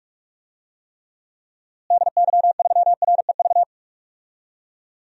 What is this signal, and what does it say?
Morse code, heard as a keyed tone, about 46 wpm, 700 Hz. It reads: DY3REV
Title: DXCC-100 Call Signs - 6 Characters - Repeated 3x - 45wpm